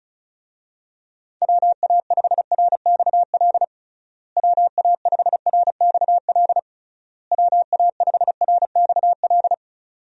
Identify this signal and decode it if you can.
Morse code, heard as a keyed tone, about 36 wpm, 700 Hz. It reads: WA5RXL WA5RXL WA5RXL
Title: USA Call Signs - 6 Characters - Repeated 3x - 35wpm